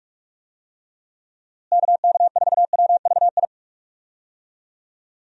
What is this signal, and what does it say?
Morse code, heard as a keyed tone, about 47 wpm, 700 Hz. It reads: KK4WVI
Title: USA Call Signs - 6 Characters - Repeated 3x - 45wpm